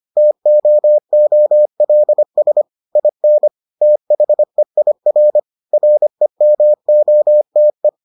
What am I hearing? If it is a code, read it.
Morse code, heard as a keyed tone, about 25 wpm, 600 Hz. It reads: TOOLS IN THEIR REMOTE